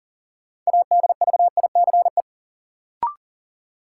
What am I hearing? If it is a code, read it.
Morse code, heard as a keyed tone, about 40 wpm, 700 Hz. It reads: ADVICE
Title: Sentences from Top 1100 Words - New Word Drill Alt - 40wpm